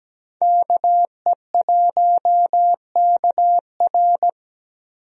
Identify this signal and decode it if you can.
Morse code, heard as a keyed tone, about 17 wpm, 700 Hz. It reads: KE1KR
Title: USA Call Signs - 5 Characters - Repeated 2x - 17wpm